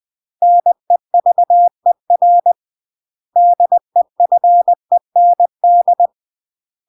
Morse code, heard as a keyed tone, about 20 wpm, 700 Hz. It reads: NEVER DEFEND
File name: CW / jodorowsky / 82 commandments 20-20